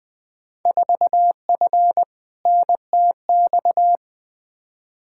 Morse code, heard as a keyed tone, about 20 wpm, 700 Hz. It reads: 4F NTX
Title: Field Day Exchanges Encore - 20wpm